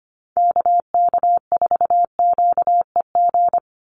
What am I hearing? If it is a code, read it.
Morse code, heard as a keyed tone, about 25 wpm, 700 Hz. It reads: KK4QEG
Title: USA Call Signs 25wpm